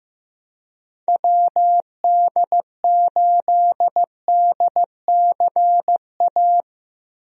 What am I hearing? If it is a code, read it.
Morse code, heard as a keyed tone, about 15 wpm, 700 Hz. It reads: WD8DCA
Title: USA Call Signs - 6 Characters - 15wpm